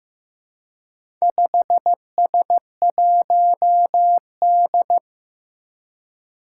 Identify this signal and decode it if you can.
Morse code, heard as a keyed tone, about 15 wpm, 700 Hz. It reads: 5S1D
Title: Four Challenging Letter-Number Combinations 15wpm